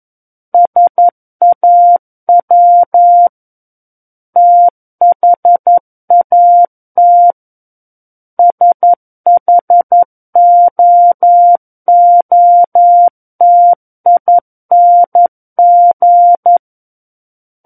Morse code, heard as a keyed tone, about 11 wpm, 700 Hz. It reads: SAW THAT SHOOTING
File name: 11 WPM